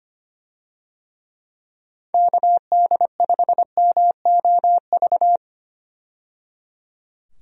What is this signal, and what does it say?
Morse code, heard as a keyed tone, about 25 wpm, 700 Hz. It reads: KD5MOV
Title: USA Call Signs - 6 Characters - 25wpm